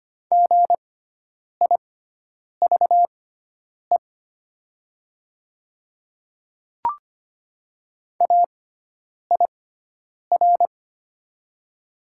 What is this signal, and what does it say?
Morse code, heard as a keyed tone, about 25 wpm, spaced as for 9 wpm, 700 Hz. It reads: GIVE AIR
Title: 6x Character Spacing - Top 200 Words - 25wpm